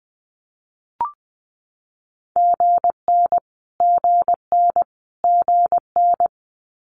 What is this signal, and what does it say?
Morse code, heard as a keyed tone, about 20 wpm, 700 Hz. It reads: GN GN GN
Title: Multi-letter Phonemes Repeated 3x 20wpm